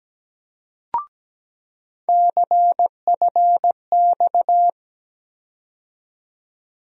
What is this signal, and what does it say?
Morse code, heard as a keyed tone, about 17 wpm, 700 Hz. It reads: CFX
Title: USA Call Sign Postfixes - 3 Letters 17wpm